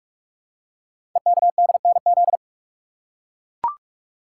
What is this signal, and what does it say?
Morse code, heard as a keyed tone, about 44 wpm, 700 Hz. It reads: EKDNZ
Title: Code Groups - 5 Letters - 45wpm